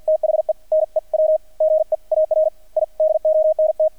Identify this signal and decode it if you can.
Morse code, heard as a keyed tone, about 50 wpm, spaced as for 44 wpm, 630 Hz. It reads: THE NEW GEAR I DONT
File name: archived / CRSnet